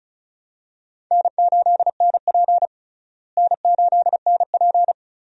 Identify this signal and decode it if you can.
Morse code, heard as a keyed tone, about 34 wpm, 700 Hz. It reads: N8NP N8NP
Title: USA Call Signs - 4 Characters - Repeated 3x - 35wpm